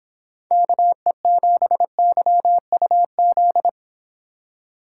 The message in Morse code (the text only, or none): KE7YUZ